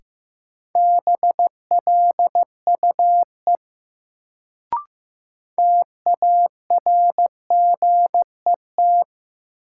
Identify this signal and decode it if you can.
Morse code, heard as a keyed tone, about 15 wpm, 700 Hz. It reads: BLUE TARGET
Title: Sentences from Top 1200 Words - New Word Drill Alt - 15wpm